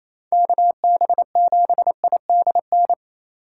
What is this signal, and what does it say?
Morse code, heard as a keyed tone, about 28 wpm, 700 Hz. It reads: KB7IDN
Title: USA Call Signs - 6 Characters - 4x Character Spacing - 28wpm